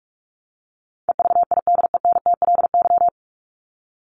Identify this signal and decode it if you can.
Morse code, heard as a keyed tone, about 44 wpm, 700 Hz. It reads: EVIDENTLY
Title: LY Suffix - Top 100 Words with Base Words 45wpm